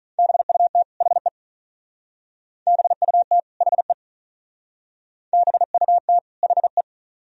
Morse code, heard as a keyed tone, about 41 wpm, 700 Hz. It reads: BUT HE BUT HE BUT HE
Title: Speed-Racing - Sets of 2 Words 35wpm